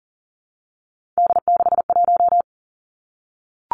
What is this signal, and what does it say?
Morse code, heard as a keyed tone, about 42 wpm, 700 Hz. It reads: D61
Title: Three Challenging Letter-Number Combinations 40wpm